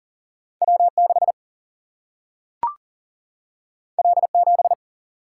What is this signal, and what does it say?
Morse code, heard as a keyed tone, about 40 wpm, 700 Hz. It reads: W6 L7
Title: Single-Letter Single-Number 40wpm